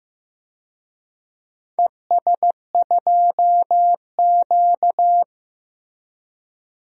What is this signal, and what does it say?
Morse code, heard as a keyed tone, about 15 wpm, 700 Hz. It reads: ES2Q
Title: DXCC-100 Call Signs - 4 Characters - Repeated 2x - 15wpm